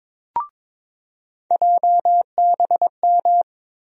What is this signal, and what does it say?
Morse code, heard as a keyed tone, about 22 wpm, 700 Hz. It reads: JBM
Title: USA Call Sign Postfixes - 3 Letters 22wpm